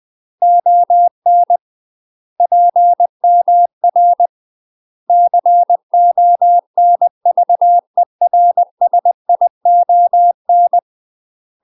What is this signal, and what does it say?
Morse code, heard as a keyed tone, about 20 wpm, 700 Hz. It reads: ON PMR CONVERSION
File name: UK MASTER QSOs TRAINING / 20WPM / UK MASTER QSOs 0010